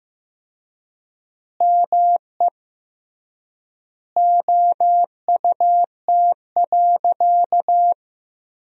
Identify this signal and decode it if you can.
Morse code, heard as a keyed tone, about 15 wpm, 700 Hz. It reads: ME OUT.
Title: 3x Word-Spacing - Sentences from Top 300 Words - 15wpm